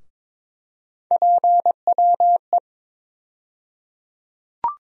Morse code, heard as a keyed tone, about 22 wpm, 700 Hz. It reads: PWE
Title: USA Call Sign Postfixes - 3 Letters 22wpm